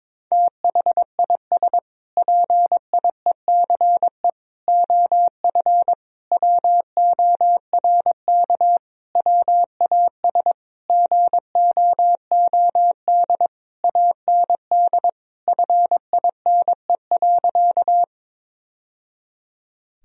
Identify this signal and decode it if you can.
Morse code, heard as a keyed tone, about 22 wpm, 700 Hz. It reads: THIS PIECE OF WORK WAS GOOD AND FINE.
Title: Sentences from Top 1000 Words - Review Part 2 - 22wpm